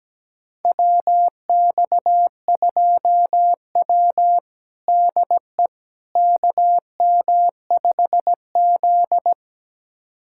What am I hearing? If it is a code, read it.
Morse code, heard as a keyed tone, about 17 wpm, 700 Hz. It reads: WX2W DE KM5Z
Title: USA Callsign DE Callsign - 4 Characters - 17wpm